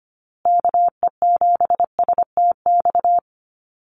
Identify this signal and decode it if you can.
Morse code, heard as a keyed tone, about 25 wpm, 700 Hz. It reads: KE7STX
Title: USA Call Signs Repeated 2x 25wpm